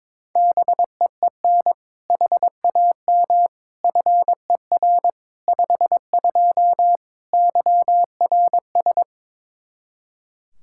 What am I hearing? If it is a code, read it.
Morse code, heard as a keyed tone, about 22 wpm, 700 Hz. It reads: BEEN HAM FER 52 YRS
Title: Practice QSO - Exchange 3 - More Info - 22wpm